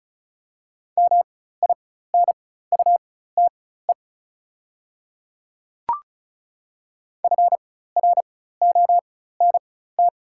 Morse code, heard as a keyed tone, about 36 wpm, spaced as for 16 wpm, 700 Hz. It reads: MINUTE FRONT
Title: 4x Character Spacing - Top 500 Words - 35wpm